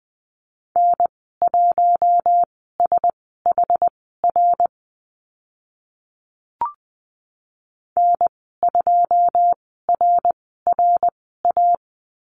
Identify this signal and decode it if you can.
Morse code, heard as a keyed tone, about 20 wpm, spaced as for 14 wpm, 700 Hz. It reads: N1SHR N2RRA
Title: 082 Lesson 14g - Number 2 - Call Signs 20wpm